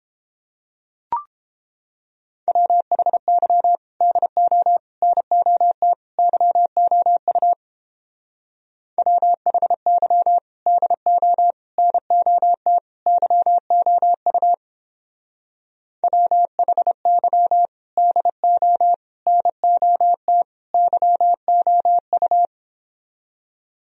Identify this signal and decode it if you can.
Morse code, heard as a keyed tone, about 31 wpm, 700 Hz. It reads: WHY DO NOT YOU WHY DO NOT YOU WHY DO NOT YOU